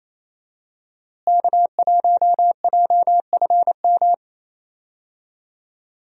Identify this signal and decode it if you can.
Morse code, heard as a keyed tone, about 28 wpm, 700 Hz. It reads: K1JFM
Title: USA Call Signs - 5 Characters - Repeated 2x - 28wpm